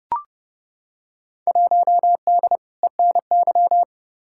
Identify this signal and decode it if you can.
Morse code, heard as a keyed tone, about 30 wpm, 700 Hz. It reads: 1D ENY
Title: Field Day Exchanges - 30wpm